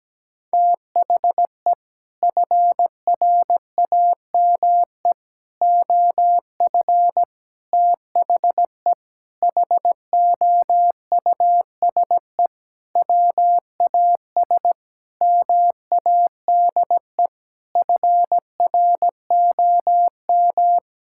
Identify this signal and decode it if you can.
Morse code, heard as a keyed tone, about 17 wpm, 700 Hz. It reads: THE FRAME OF THE HOUSE WAS MADE FROM
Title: Sentences from Top 1000 Words - Review Part 6 - 17wpm